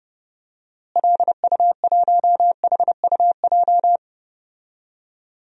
Morse code, heard as a keyed tone, about 32 wpm, 700 Hz. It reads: LU1HUJ
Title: DXCC-100 Call Signs - 6 Characters - Repeated 3x - 30wpm